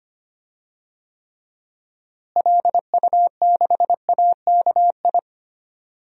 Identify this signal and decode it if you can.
Morse code, heard as a keyed tone, about 25 wpm, 700 Hz. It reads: LU6AKI
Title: DXCC-100 Call Signs - 6 Characters - Repeated 3x - 25wpm